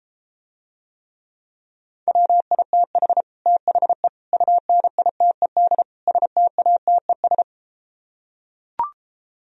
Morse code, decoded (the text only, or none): WITH THE UNITED STATES